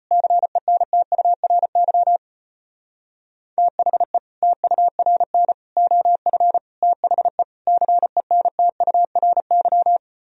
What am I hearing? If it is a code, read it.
Morse code, heard as a keyed tone, about 33 wpm, 700 Hz. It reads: CENTURY THE TURN OF THE CENTURY